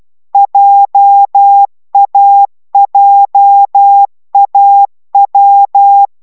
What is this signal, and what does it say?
Morse code, heard as a keyed tone, about 12 wpm, 800 Hz. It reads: JAJAW